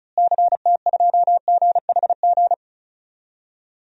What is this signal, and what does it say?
Morse code, heard as a keyed tone, about 36 wpm, 700 Hz. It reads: CT2GHG
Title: DXCC-100 Call Signs - 6 Characters - Repeated 3x - 35wpm